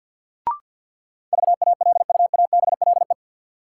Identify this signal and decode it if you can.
Morse code, heard as a keyed tone, about 48 wpm, 700 Hz. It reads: VALUABLE
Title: Sentences from Top 1100 Words - New Word Drill Alt - 50wpm